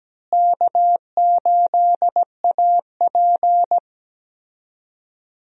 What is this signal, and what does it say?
Morse code, heard as a keyed tone, about 17 wpm, 700 Hz. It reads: K8AP
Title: USA Call Signs - 4 Characters - Repeated 2x - 17wpm